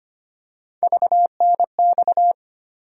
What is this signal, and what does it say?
Morse code, heard as a keyed tone, about 25 wpm, 700 Hz. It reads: VNX
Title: USA Call Sign Postfixes - 3 Letters 25wpm